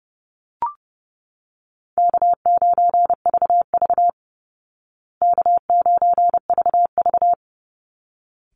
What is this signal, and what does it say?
Morse code, heard as a keyed tone, about 30 wpm, 700 Hz. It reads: K9VV K9VV
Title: Call Signs Repeated 3x - 30wpm